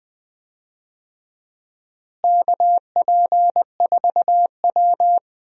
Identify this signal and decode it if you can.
Morse code, heard as a keyed tone, about 21 wpm, 700 Hz. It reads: KP4W
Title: DXCC-100 Call Signs - 4 Characters - Repeated 3x - 20wpm